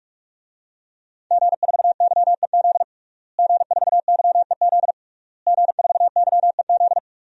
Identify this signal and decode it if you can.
Morse code, heard as a keyed tone, about 46 wpm, 700 Hz. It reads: G4YEZ G4YEZ G4YEZ
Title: DXCC-100 Call Signs - 5 Characters - Repeated 3x - 45wpm